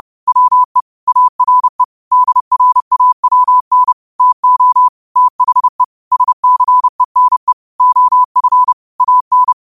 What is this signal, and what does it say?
Morse code, heard as a keyed tone, about 30 wpm, 1000 Hz. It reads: WE ARE DRAWN TO THE SCENE OF AN
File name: qotdfiles / 30wpm